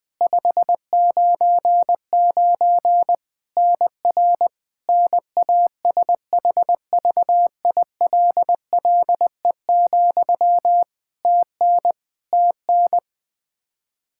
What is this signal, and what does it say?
Morse code, heard as a keyed tone, about 21 wpm, 700 Hz. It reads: 599 NR NASHVILLE, TN TN
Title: Practice QSO - Exchange 1 - Beginning - 20wpm